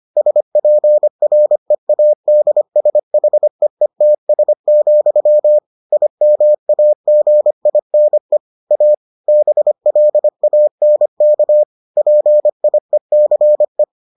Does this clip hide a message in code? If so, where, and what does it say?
Yes: Morse code from 0.2 to 13.8 s, heard as a keyed tone, about 25 wpm, 600 Hz. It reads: SPREADSHEETS, IMAGINE A BLANK PIECE